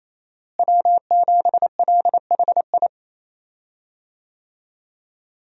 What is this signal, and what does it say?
Morse code, heard as a keyed tone, about 28 wpm, 700 Hz. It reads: W7LHI
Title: USA Call Signs - 5 Characters - Repeated 3x - 28wpm